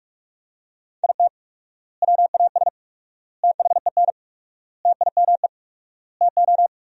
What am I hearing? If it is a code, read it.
Morse code, heard as a keyed tone, about 46 wpm, 700 Hz. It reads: IT WAS THEN TIME TO
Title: 4x Word-Spacing - Sentences from Top 300 Words - 45wpm